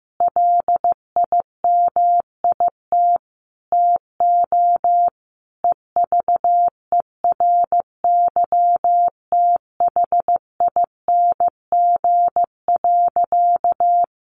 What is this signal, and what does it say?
Morse code, heard as a keyed tone, about 15 wpm, 700 Hz. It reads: LIMIT TO EVERYTHING.
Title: Sentences from Top 1500 Words 15wpm